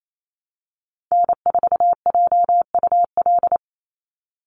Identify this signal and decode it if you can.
Morse code, heard as a keyed tone, about 28 wpm, 700 Hz. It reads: N4JUL
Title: USA Call Signs 28wpm